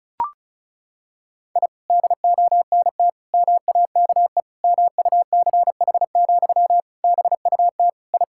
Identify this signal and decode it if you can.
Morse code, heard as a keyed tone, about 36 wpm, 700 Hz. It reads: I DONT MAKE MUCH, BUT I
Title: Sentences from Top 1000 Words - Review Part 5 - 35wpm